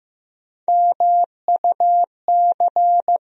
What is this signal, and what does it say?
Morse code, heard as a keyed tone, about 15 wpm, 700 Hz. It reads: MUC
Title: USA Call Sign Postfixes - 3 Letters 15wpm